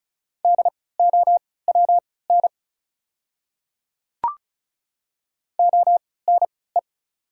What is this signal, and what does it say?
Morse code, heard as a keyed tone, about 36 wpm, spaced as for 20 wpm, 690 Hz. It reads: DOWN ONE
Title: 3x Character Spacing - Top 100 Words - 35wpm